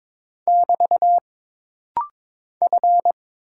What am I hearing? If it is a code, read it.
Morse code, heard as a keyed tone, about 22 wpm, 700 Hz. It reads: <BT> F
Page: This rapid-fire timing is geared to push your ICR (Instant Character Recognition) skills.